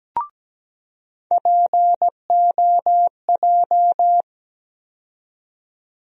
Morse code, heard as a keyed tone, about 17 wpm, 700 Hz. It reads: POJ
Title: USA Call Sign Postfixes - 3 Letters 17wpm